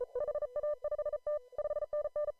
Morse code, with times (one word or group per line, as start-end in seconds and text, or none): 0.2-1.4 HA5T
1.6-2.3 5NN